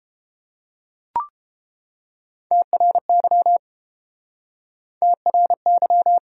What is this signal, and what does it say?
Morse code, heard as a keyed tone, about 30 wpm, 700 Hz. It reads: TRY TRY